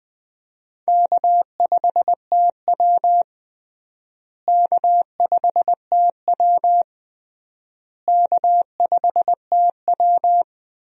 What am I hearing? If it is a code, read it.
Morse code, heard as a keyed tone, about 20 wpm, 700 Hz. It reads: K5TW K5TW K5TW
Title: USA Call Signs - 4 Characters - Repeated 3x - 20wpm